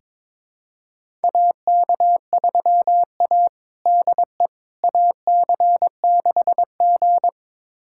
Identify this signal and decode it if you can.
Morse code, heard as a keyed tone, about 22 wpm, 700 Hz. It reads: AK3A DE AC6G
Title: USA Callsign DE Callsign - 4 Characters - 22wpm